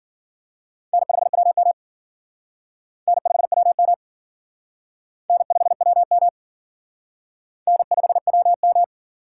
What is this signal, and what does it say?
Morse code, heard as a keyed tone, about 53 wpm, 700 Hz. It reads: N5WM N5WM N5WM N5WM